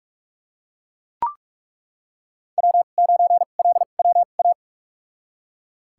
Morse code, heard as a keyed tone, about 41 wpm, spaced as for 31 wpm, 700 Hz. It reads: W9LWA
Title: USA Call Signs - 5 Characters - 2x Character Spacing - 45wpm